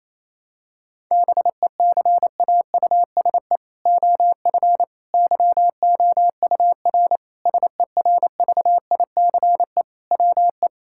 Morse code, heard as a keyed tone, about 28 wpm, 700 Hz. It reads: BECAUSE OF YOUR SERVICE WE
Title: Sentences from Top 1000 Words - Review Part 5 - 28wpm